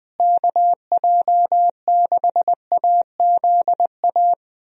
Morse code, heard as a keyed tone, about 20 wpm, 700 Hz. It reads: KJ6AZA
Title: USA Call Signs - 6 Characters - Repeated 3x - 20wpm